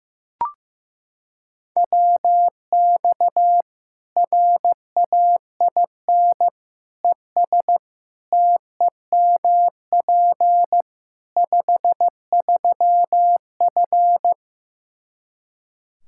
Morse code, heard as a keyed tone, about 15 wpm, 700 Hz. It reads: WX RAIN ES TEMP 53F
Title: Practice QSO - Exchange 2 - Info - 15wpm